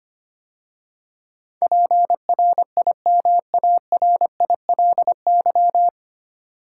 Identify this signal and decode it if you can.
Morse code, heard as a keyed tone, about 25 wpm, 690 Hz. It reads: PRIMARILY
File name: practice / 25wpm / 9 Letter Words - First 3 Letters Given - 25wpm